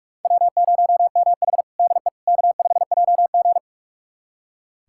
Morse code, heard as a keyed tone, about 46 wpm, 700 Hz. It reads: W0MH DE K5JG
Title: USA Callsign DE Callsign - 4 Characters - 45wpm